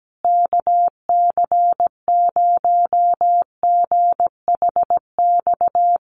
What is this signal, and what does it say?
Morse code, heard as a keyed tone, about 17 wpm, 700 Hz. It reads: KC0GHX
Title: USA Call Signs Repeated 3x 17wpm